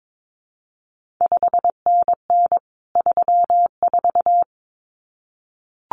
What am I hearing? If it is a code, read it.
Morse code, heard as a keyed tone, about 22 wpm, 700 Hz. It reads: 5NN 34
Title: CQ WW DX Contest Exchange Encore - 22wpm